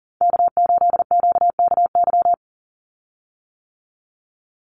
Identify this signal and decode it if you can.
Morse code, heard as a keyed tone, about 40 wpm, 700 Hz. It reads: K8QKY
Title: Call Signs - 40wpm